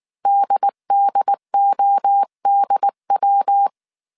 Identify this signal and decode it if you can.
Morse code, heard as a keyed tone, about 19 wpm, 800 Hz. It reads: DDODW